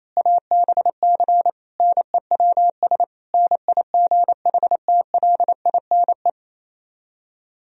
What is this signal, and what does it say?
Morse code, heard as a keyed tone, about 27 wpm, 700 Hz. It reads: ABC NEWS NIGHTLINE
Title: Speed-Racing - Sets of 5 Words 28wpm